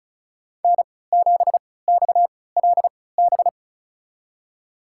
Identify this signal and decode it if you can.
Morse code, heard as a keyed tone, about 36 wpm, spaced as for 20 wpm, 700 Hz. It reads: N7XLB
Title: USA Call Signs - 5 Characters - 3x Character Spacing - 35wpm